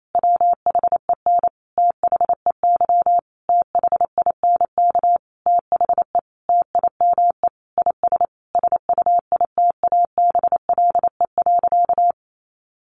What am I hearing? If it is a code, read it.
Morse code, heard as a keyed tone, about 28 wpm, 700 Hz. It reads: WHEN THEY THINK THE TIME IS SUITABLE.
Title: Sentences from Top 1400 Words 28wpm